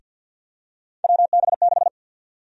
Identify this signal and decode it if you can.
Morse code, heard as a keyed tone, about 53 wpm, 700 Hz. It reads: WB6
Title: USA Call Sign Prefixes - Repeated 2x - 50wpm